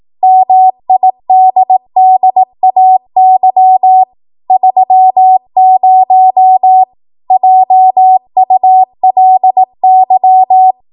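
Morse code, heard as a keyed tone, about 18 wpm, 750 Hz. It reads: MIDDAY 30 JULY
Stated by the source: 18 WPM Morse code practice from a national amateur radio association's code archive.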